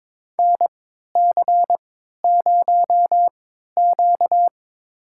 Morse code, heard as a keyed tone, about 22 wpm, spaced as for 12 wpm, 700 Hz. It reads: NC0Q
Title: USA Call Signs - 4 Characters - 3x Character Spacing - 22wpm